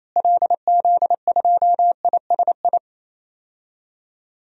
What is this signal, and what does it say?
Morse code, heard as a keyed tone, about 28 wpm, 700 Hz. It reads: LZ2ISI
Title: DXCC-100 Call Signs - 6 Characters - Repeated 3x - 28wpm